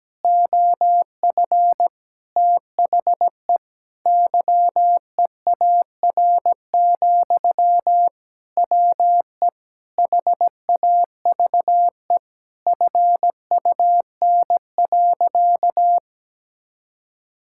Morse code, heard as a keyed tone, about 17 wpm, 700 Hz. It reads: OF THE YEAR, WE HAVE FUN.
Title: Sentences from Top 1000 Words - Review Part 5 - 17wpm